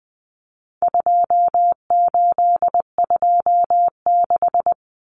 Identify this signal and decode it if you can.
Morse code, heard as a keyed tone, about 20 wpm, 700 Hz. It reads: 2826
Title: Numbers - 4-Digit 20wpm